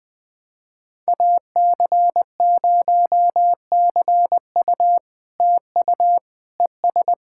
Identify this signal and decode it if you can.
Morse code, heard as a keyed tone, about 20 wpm, 700 Hz. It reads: AC0CU TU ES
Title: POTA QSO Practice - 20wpm